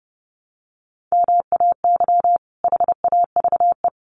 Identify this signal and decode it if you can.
Morse code, heard as a keyed tone, about 29 wpm, 700 Hz. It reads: MAY HAVE
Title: Sets of 2 Words 30wpm